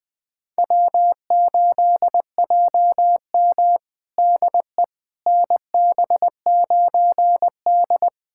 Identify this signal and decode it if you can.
Morse code, heard as a keyed tone, about 20 wpm, 700 Hz. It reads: W8JM DE NB9D
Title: USA Callsign DE Callsign - 4 Characters - 20wpm